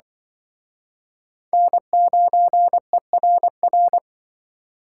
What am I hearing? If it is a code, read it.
Morse code, heard as a keyed tone, about 24 wpm, 700 Hz. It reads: N9ERR
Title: Speed Racing - USA Call Signs 20wpm